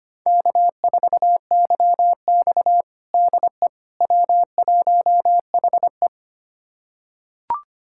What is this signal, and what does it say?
Morse code, heard as a keyed tone, about 25 wpm, 700 Hz. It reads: K4YX DE W1HE
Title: USA Callsign DE Callsign - 4 Characters - 25wpm